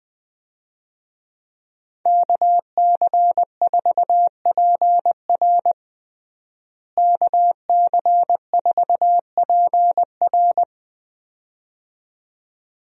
Morse code, heard as a keyed tone, about 20 wpm, 700 Hz. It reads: KC4PR KC4PR
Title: USA Call Signs - 5 Characters - Repeated 2x - 20wpm